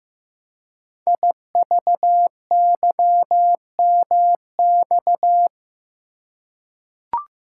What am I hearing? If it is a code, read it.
Morse code, heard as a keyed tone, about 15 wpm, 700 Hz. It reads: IVYMX
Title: Code Groups - 5 Letters - 15wpm